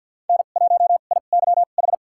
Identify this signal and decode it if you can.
Morse code, heard as a keyed tone, about 52 wpm, spaced as for 36 wpm, 700 Hz. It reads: N1IYH
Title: USA Call Signs - 5 Characters - 2x Character Spacing - 50wpm